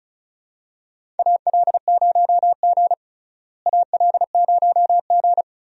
Morse code, heard as a keyed tone, about 36 wpm, 700 Hz. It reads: AL0G AL0G
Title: USA Call Signs - 4 Characters - Repeated 2x - 35wpm